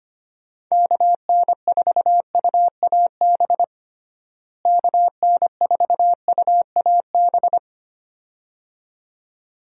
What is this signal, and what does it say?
Morse code, heard as a keyed tone, about 25 wpm, 700 Hz. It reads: KN4UAB KN4UAB
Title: USA Call Signs - 6 Characters - Repeated 2x - 25wpm